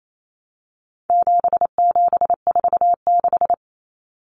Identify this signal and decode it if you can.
Morse code, heard as a keyed tone, about 28 wpm, 700 Hz. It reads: 7746
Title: Numbers - 4-Digit 28wpm